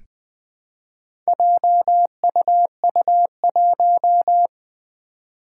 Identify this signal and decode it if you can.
Morse code, heard as a keyed tone, about 20 wpm, 700 Hz. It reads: JUU1
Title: Four Challenging Letter-Number Combinations 20wpm